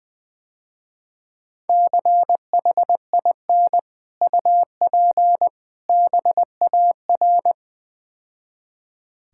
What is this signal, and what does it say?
Morse code, heard as a keyed tone, about 20 wpm, 700 Hz. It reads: CHIN UP BAR
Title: Difficult Words - Speed-Racing with Hints - 20wpm